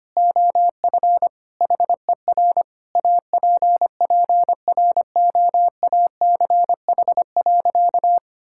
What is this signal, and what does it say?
Morse code, heard as a keyed tone, about 25 wpm, 700 Hz. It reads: OF HER APPROACH.
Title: Sentences from Top 1000 Words - Review Part 6 - 25wpm